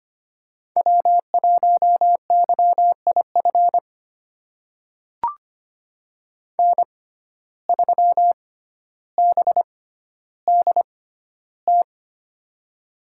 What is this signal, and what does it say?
Morse code, heard as a keyed tone, about 25 wpm, spaced as for 13 wpm, 700 Hz. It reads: W1YIF N3BDT
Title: USA Call Signs - 5 Characters - 6x Character Spacing - 25wpm